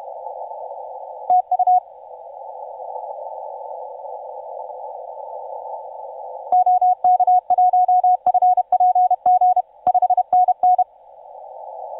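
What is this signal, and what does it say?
Morse code, heard as a keyed tone, about 31 wpm, 730 Hz. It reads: TU OK1FPG 5NN
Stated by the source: TU7C on 20 metre CW, DX station recording by IZ8GNR.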